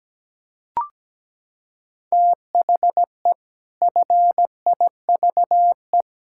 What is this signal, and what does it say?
Morse code, heard as a keyed tone, about 17 wpm, 700 Hz. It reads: THE FIVE
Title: Pangram Challenge v2 - 17wpm